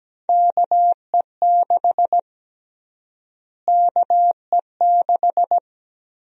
Morse code, heard as a keyed tone, about 17 wpm, 700 Hz. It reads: KE6 KE6
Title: USA Call Sign Prefixes - Repeated 2x - 17wpm